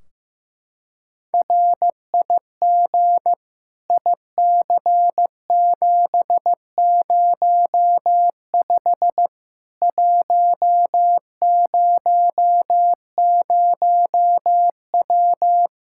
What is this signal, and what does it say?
Morse code, heard as a keyed tone, about 15 wpm, 700 Hz. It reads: RIG IC705 100W
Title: K1USN SST QSO Practice 15wpm